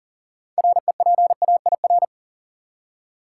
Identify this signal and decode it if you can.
Morse code, heard as a keyed tone, about 40 wpm, 700 Hz. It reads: REPAIR
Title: Six Letter Words - 40wpm